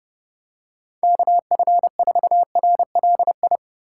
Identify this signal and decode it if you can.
Morse code, heard as a keyed tone, about 29 wpm, 700 Hz. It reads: KF4RLI